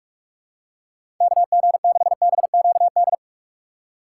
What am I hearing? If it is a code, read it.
Morse code, heard as a keyed tone, about 46 wpm, 700 Hz. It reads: KG6BQD
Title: USA Call Signs - 6x Character Spacing - 45wpm